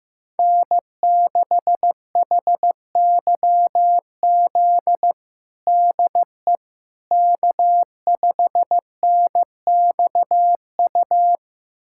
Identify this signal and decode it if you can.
Morse code, heard as a keyed tone, about 15 wpm, 700 Hz. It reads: N6HYZ DE K5NXU
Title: USA Callsign DE Callsign - 5 Characters - 15wpm